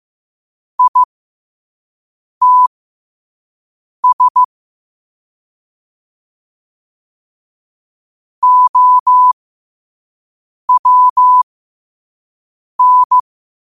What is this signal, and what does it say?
Morse code, heard as a keyed tone, about 15 wpm, spaced as for 5 wpm, 1000 Hz. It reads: ITS OWN